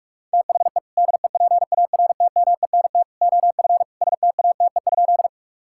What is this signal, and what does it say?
Morse code, heard as a keyed tone, about 41 wpm, 700 Hz. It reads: THE DEPARTMENT OF STATE?
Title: Sentences from Top 1000 Words - Review Part 5 - 45wpm